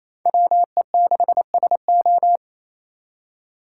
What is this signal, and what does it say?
Morse code, heard as a keyed tone, about 36 wpm, 700 Hz. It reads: WE6SO